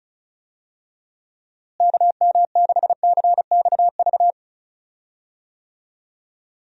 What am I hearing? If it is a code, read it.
Morse code, heard as a keyed tone, about 36 wpm, 700 Hz. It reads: KM6CXV
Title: USA Call Signs - 6 Characters - Repeated 3x - 35wpm